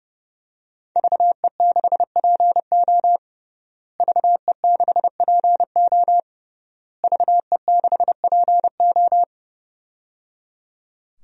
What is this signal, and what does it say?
Morse code, heard as a keyed tone, about 30 wpm, 700 Hz. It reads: VE6PO VE6PO VE6PO
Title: DXCC-100 Call Signs - 5 Characters - Repeated 3x - 30wpm